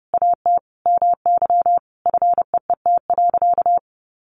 Morse code, heard as a keyed tone, about 30 wpm, 700 Hz. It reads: AT MY FEET.
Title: Sentences from Top 1000 Words 30wpm